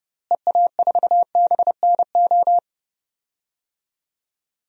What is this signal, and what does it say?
Morse code, heard as a keyed tone, about 30 wpm, 700 Hz. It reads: EA4BNO
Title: DXCC-100 Call Signs - 6 Characters - Repeated 2x - 30wpm